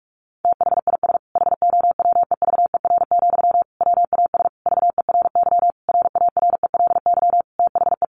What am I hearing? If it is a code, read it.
Morse code, heard as a keyed tone, about 44 wpm, 700 Hz. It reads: THIS HOWEVER, WAS VERY RARELY THE
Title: Sentences from Top 1400 Words 45wpm